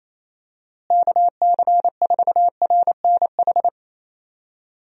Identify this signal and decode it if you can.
Morse code, heard as a keyed tone, about 28 wpm, 700 Hz. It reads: KC4RNH
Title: USA Call Signs - 6 Characters - Repeated 3x - 28wpm